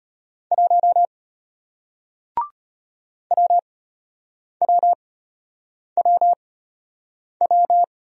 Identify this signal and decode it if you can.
Morse code, heard as a keyed tone, about 33 wpm, spaced as for 8 wpm, 700 Hz. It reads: 1 WWWW